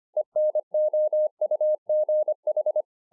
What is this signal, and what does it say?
Morse code, heard as a keyed tone, about 25 wpm, 610 Hz. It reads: ENOUGH